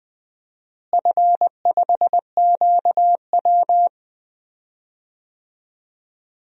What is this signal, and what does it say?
Morse code, heard as a keyed tone, about 20 wpm, 700 Hz. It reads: F5QW
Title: DXCC-100 Call Signs - 4 Characters - Repeated 3x - 20wpm